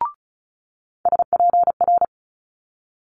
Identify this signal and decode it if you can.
Morse code, heard as a keyed tone, about 36 wpm, 700 Hz. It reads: SPR
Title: Multi-letter Phonemes Quick Fire 35wpm